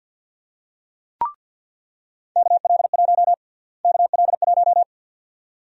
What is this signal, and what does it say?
Morse code, heard as a keyed tone, about 53 wpm, 700 Hz. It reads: KL1 KL1
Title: USA Call Sign Prefixes - Repeated 2x - 50wpm